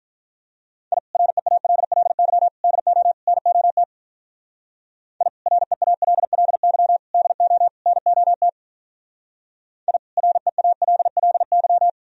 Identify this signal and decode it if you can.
Morse code, heard as a keyed tone, about 45 wpm, 700 Hz. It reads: I REALLY DO NOT I REALLY DO NOT I REALLY